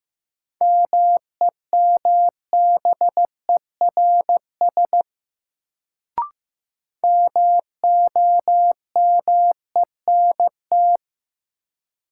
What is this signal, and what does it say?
Morse code, heard as a keyed tone, about 15 wpm, 700 Hz. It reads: MEMBERS MOMENT
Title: Sets of 3 Words - New Word Drill - 15wpm